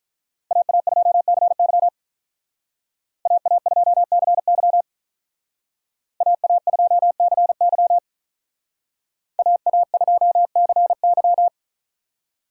Morse code, heard as a keyed tone, about 42 wpm, 700 Hz. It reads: AA2CY AA2CY AA2CY AA2CY
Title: Speed Racing - USA 2x2 Call Signs 35wpm